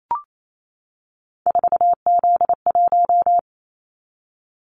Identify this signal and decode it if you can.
Morse code, heard as a keyed tone, about 28 wpm, 700 Hz. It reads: 4Z1
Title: Callsign Prefixes for Top 50 DXCC - 28wpm